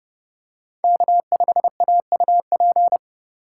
Morse code, heard as a keyed tone, about 30 wpm, 700 Hz. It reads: K5AUP